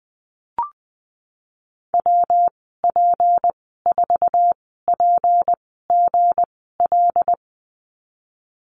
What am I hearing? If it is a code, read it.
Morse code, heard as a keyed tone, about 20 wpm, spaced as for 14 wpm, 700 Hz. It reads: WP4PGL
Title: 173 Lesson 28g - Number 9 - Call Signs 20wpm